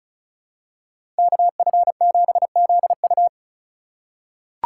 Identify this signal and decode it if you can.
Morse code, heard as a keyed tone, about 34 wpm, 700 Hz. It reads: KF7ZU
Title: USA Call Signs - 4x Character Spacing - 35wpm